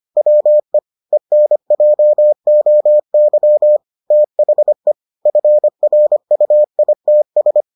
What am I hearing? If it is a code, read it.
Morse code, heard as a keyed tone, about 25 wpm, 600 Hz. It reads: WE ENJOY THE FRUITS